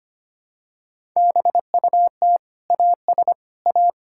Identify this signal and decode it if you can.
Morse code, heard as a keyed tone, about 25 wpm, 700 Hz. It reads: BUT AS A
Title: Sentences from Top 1000 Words - Review Part 6 - 25wpm